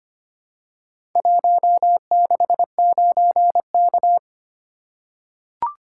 Morse code, heard as a keyed tone, about 26 wpm, 700 Hz. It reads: 169K
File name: practice / 25wpm / Four Letter-Number Combinations - 25wpm